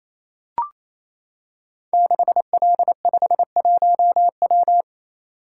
Morse code, heard as a keyed tone, about 27 wpm, 700 Hz. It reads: 6L51W
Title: Code Groups - 5 Letter:Numbers - 28wpm